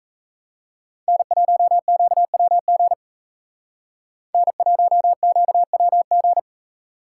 Morse code, heard as a keyed tone, about 34 wpm, 700 Hz. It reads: N1QWG N1QWG
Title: Speed Racing - USA 1x3 Call Signs 28wpm